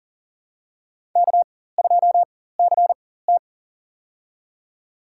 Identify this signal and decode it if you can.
Morse code, heard as a keyed tone, about 40 wpm, spaced as for 19 wpm, 700 Hz. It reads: K2CT
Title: USA Call Signs - 4 Characters - 4x Character Spacing - 40wpm